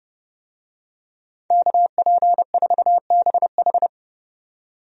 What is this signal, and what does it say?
Morse code, heard as a keyed tone, about 30 wpm, 700 Hz. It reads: KP4BH
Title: DXCC-100 Call Signs - 5 Characters - 30wpm